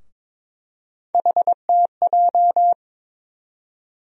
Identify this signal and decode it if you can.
Morse code, heard as a keyed tone, about 22 wpm, 700 Hz. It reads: HTJ
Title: USA Call Sign Postfixes - 3 Letters 22wpm